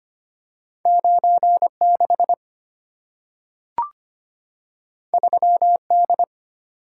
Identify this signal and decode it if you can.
Morse code, heard as a keyed tone, about 25 wpm, 690 Hz. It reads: 96 3D
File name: Two Letter-Number Combinations - 25wpm